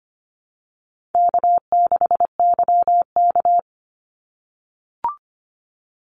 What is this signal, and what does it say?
Morse code, heard as a keyed tone, about 25 wpm, 700 Hz. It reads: K6YK
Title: NA SOTA Chaser Callsigns 25wpm